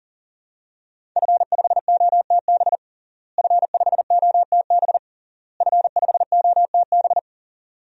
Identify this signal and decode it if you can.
Morse code, heard as a keyed tone, about 37 wpm, 700 Hz. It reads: F5OTB F5OTB F5OTB
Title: DXCC-100 Call Signs - 5 Characters - Repeated 3x - 40wpm